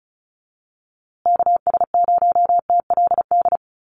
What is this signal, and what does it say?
Morse code, heard as a keyed tone, about 34 wpm, 700 Hz. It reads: KS0TLD
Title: USA Call Signs Repeated 3x 35wpm